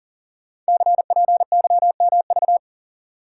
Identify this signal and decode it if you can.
Morse code, heard as a keyed tone, about 42 wpm, 700 Hz. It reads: CPYMV
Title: Code Groups - 5 Letter:Numbers - 40wpm